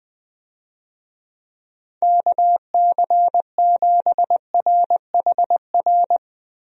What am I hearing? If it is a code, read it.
Morse code, heard as a keyed tone, about 20 wpm, 700 Hz. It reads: KC7RHR
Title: USA Call Signs - 6 Characters - Repeated 3x - 20wpm